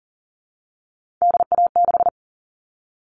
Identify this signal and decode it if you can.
Morse code, heard as a keyed tone, about 40 wpm, 700 Hz. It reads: DA6
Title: All Cut Numbers - 3-Digit 40wpm